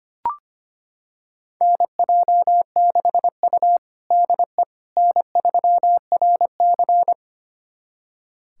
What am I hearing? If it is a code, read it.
Morse code, heard as a keyed tone, about 25 wpm, 700 Hz. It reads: NJ6U DE N3RC
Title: USA Callsign DE Callsign - 4 Characters - 25wpm